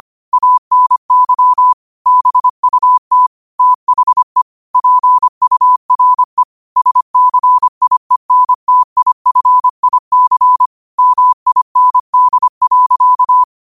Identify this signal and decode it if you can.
Morse code, heard as a keyed tone, about 25 wpm, 1000 Hz. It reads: ANY BUT THE PURE SCIENTIFIC MIND.
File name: qotdfiles / 25wpm